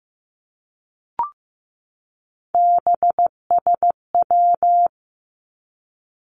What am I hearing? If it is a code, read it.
Morse code, heard as a keyed tone, about 15 wpm, 700 Hz. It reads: BSW
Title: Three Challenging Letter-Number Combinations 15wpm